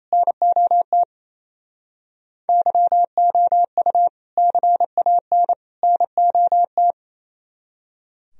Morse code, heard as a keyed tone, about 36 wpm, 700 Hz. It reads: NOT YOU CAN NOT